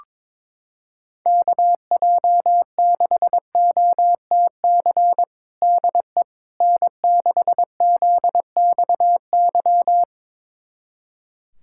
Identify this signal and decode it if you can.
Morse code, heard as a keyed tone, about 22 wpm, 700 Hz. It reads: KJ6OTC DE N6ZXY
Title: USA Callsign DE Callsign 22wpm